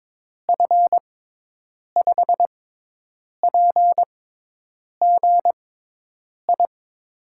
Morse code, heard as a keyed tone, about 22 wpm, spaced as for 8 wpm, 700 Hz. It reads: F5PGI
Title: USA Call Signs - 6 Characters - 6x Character Spacing - 22wpm